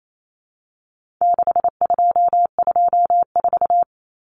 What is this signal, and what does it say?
Morse code, heard as a keyed tone, about 28 wpm, 700 Hz. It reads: 6224
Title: Common Cut Numbers - 4-Digit 28wpm